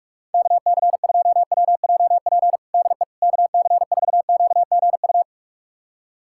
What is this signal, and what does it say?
Morse code, heard as a keyed tone, about 46 wpm, 700 Hz. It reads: KC2WJP DE KC4QGU
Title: USA Callsign DE Callsign - 6 Characters - 45wpm